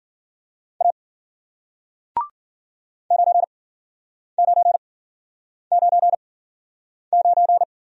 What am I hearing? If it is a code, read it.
Morse code, heard as a keyed tone, about 54 wpm, spaced as for 9 wpm, 700 Hz. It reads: A 9999